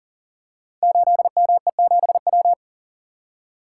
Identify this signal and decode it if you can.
Morse code, heard as a keyed tone, about 38 wpm, 690 Hz. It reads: 8ME7W
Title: Code Groups - 5 Letter:Numbers - 40wpm